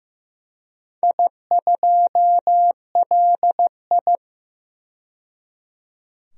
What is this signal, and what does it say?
Morse code, heard as a keyed tone, about 15 wpm, 700 Hz. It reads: I2LI
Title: DXCC-100 Call Signs - 4 Characters - 15wpm